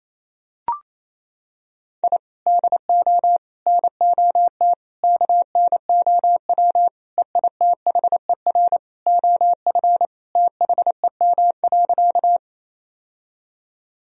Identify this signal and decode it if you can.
Morse code, heard as a keyed tone, about 28 wpm, 700 Hz. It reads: I DO NOT KNOW EITHER OF THEM.
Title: Sentences from Top 1000 Words - Review Part 4 - 28wpm